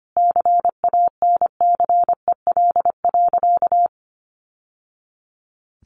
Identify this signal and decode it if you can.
Morse code, heard as a keyed tone, about 25 wpm, 700 Hz. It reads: CANCEL.
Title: Sentences from Top 1200 Words 25wpm